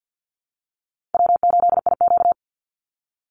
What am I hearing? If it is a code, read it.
Morse code, heard as a keyed tone, about 50 wpm, 700 Hz. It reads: W8IQ
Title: Call Signs Repeated 2x - 50wpm